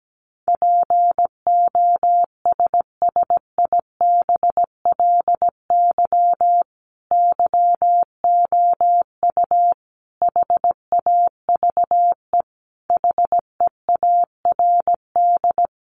Morse code, heard as a keyed tone, about 17 wpm, 700 Hz. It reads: POSSIBLY YOU HAVE HEARD
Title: Sentences from Top 1300 Words 17wpm